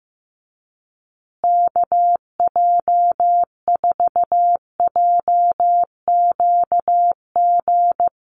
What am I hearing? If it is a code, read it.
Morse code, heard as a keyed tone, about 15 wpm, 700 Hz. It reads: KJ4JQG
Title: USA Call Signs Repeated 3x 15wpm